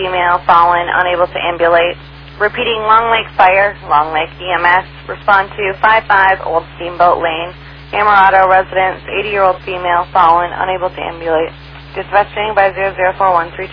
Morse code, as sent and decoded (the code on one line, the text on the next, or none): none